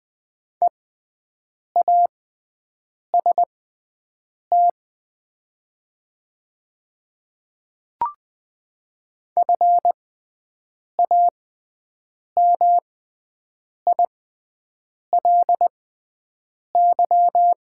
Morse code, heard as a keyed tone, about 21 wpm, spaced as for 7 wpm, 700 Hz. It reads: EAST FAMILY
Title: 6x Character Spacing - Top 500 Words - 20wpm